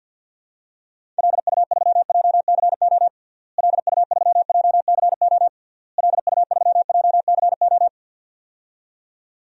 Morse code, heard as a keyed tone, about 52 wpm, 700 Hz. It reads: LU3JCO LU3JCO LU3JCO
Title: DXCC-100 Call Signs - 6 Characters - Repeated 3x - 50wpm